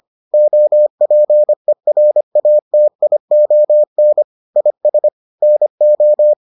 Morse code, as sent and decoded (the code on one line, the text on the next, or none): --- .--. .-. .- - .. --- -. / .. ... / -. ---
OPERATION IS NO